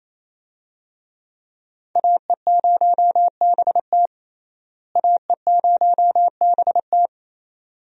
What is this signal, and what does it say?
Morse code, heard as a keyed tone, about 28 wpm, 700 Hz. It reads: AE0BT AE0BT
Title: USA Call Signs - 5 Characters - Repeated 2x - 28wpm